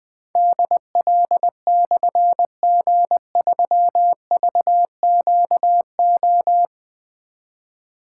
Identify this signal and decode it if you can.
Morse code, heard as a keyed tone, about 20 wpm, 700 Hz. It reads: DL/G3VQO
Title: Difficult Call Signs 20wpm